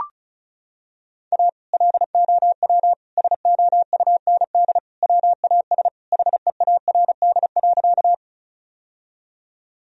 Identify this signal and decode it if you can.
Morse code, heard as a keyed tone, about 34 wpm, 700 Hz. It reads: A LOW SOUND WAS HEARD.
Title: Sentences from Top 1000 Words - Review Part 1 - 35wpm